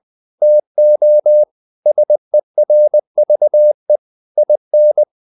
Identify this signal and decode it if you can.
Morse code, heard as a keyed tone, about 20 wpm, 600 Hz. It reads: TO SERVE IN
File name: the old man and the sea / the old man and the sea part 1 20wpm